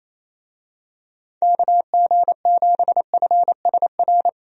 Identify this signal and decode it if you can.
Morse code, heard as a keyed tone, about 28 wpm, 700 Hz. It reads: KG7FSR
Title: USA Call Signs - 6 Characters - Repeated 2x - 28wpm